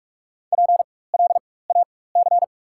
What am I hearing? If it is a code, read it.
Morse code, heard as a keyed tone, about 46 wpm, spaced as for 21 wpm, 690 Hz. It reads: PLAC
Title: 4x Character Spacing - Top 200 Words - 45wpm